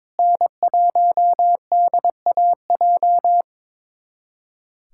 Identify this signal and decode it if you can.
Morse code, heard as a keyed tone, about 22 wpm, 700 Hz. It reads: N1DAJ
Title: USA Callsign DE Callsign 22wpm